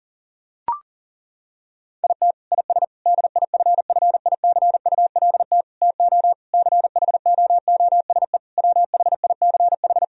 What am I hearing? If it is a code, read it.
Morse code, heard as a keyed tone, about 40 wpm, 700 Hz. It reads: IT IS DIFFICULT TO CHOOSE WHICH
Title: Sentences from Top 1000 Words - Review Part 3 - 40wpm